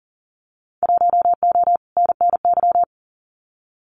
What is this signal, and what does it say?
Morse code, heard as a keyed tone, about 40 wpm, 700 Hz. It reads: 1O NNY
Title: Winter Field Day Exchanges Encore 40wpm